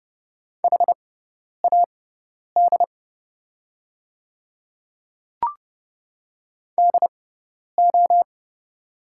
Morse code, heard as a keyed tone, about 30 wpm, spaced as for 10 wpm, 700 Hz. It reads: HAD DO
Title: 6x Character Spacing - Top 100 Words - 30wpm